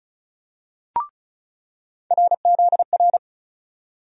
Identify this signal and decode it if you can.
Morse code, heard as a keyed tone, about 34 wpm, 700 Hz. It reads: RZR
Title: USA Call Sign Postfixes - 3 Letters 35wpm